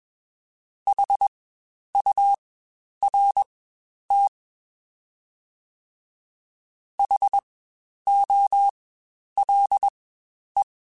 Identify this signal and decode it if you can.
Morse code, heard as a keyed tone, about 20 wpm, spaced as for 10 wpm, 800 Hz. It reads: HURT HOLE
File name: Spaced Alphas / Spaced Alphas mp3s / Hs sp @20